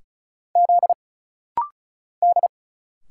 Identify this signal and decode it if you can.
Morse code, heard as a keyed tone, about 37 wpm, 700 Hz. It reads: Z D
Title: Single Letters - Rapid-Fire 35wpm